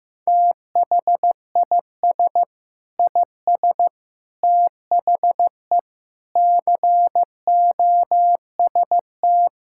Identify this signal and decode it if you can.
Morse code, heard as a keyed tone, about 15 wpm, 700 Hz. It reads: THIS IS THE COST
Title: Sentences from Top 1000 Words - Review Part 5 - 15wpm